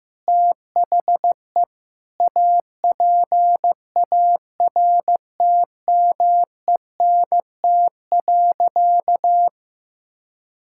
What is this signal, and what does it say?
Morse code, heard as a keyed tone, about 15 wpm, 700 Hz. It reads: THE APARTMENT.
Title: Sentences from Top 1000 Words - Review Part 6 - 15wpm